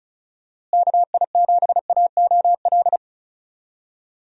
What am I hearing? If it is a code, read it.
Morse code, heard as a keyed tone, about 36 wpm, 700 Hz. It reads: KI7AOL
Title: USA Call Signs - 6 Characters - Repeated 3x - 35wpm